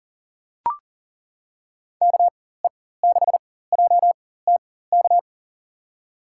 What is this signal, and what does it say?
Morse code, heard as a keyed tone, about 40 wpm, spaced as for 19 wpm, 700 Hz. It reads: KE6JTK
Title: USA Call Signs - 4x Character Spacing - 40wpm